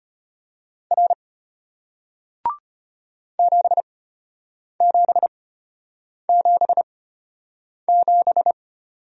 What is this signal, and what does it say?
Morse code, heard as a keyed tone, about 34 wpm, spaced as for 8 wpm, 700 Hz. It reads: R 7777